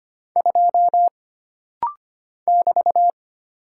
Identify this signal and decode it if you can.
Morse code, heard as a keyed tone, about 25 wpm, 700 Hz. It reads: 2 <BT>
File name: Single Character (Letter:Number:Punctuation:Prosign) - Rapid-Fire - 25wpm